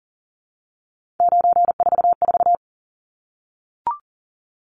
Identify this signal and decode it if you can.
Morse code, heard as a keyed tone, about 40 wpm, 700 Hz. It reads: 944
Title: Numbers - 3-Digit 40wpm